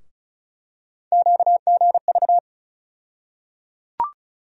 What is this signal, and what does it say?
Morse code, heard as a keyed tone, about 36 wpm, 700 Hz. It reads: QGV
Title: USA Call Sign Postfixes - 3 Letters 35wpm